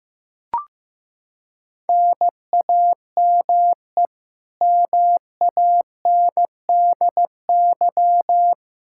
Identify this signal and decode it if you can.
Morse code, heard as a keyed tone, about 15 wpm, 700 Hz. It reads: NAME MANDY
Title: Practice QSO - Exchange 1 - Beginning - 15wpm